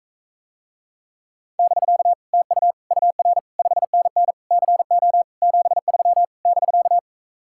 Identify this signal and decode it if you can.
Morse code, heard as a keyed tone, about 44 wpm, 700 Hz. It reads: <BK> TU UR 5NN CO 73 <BK>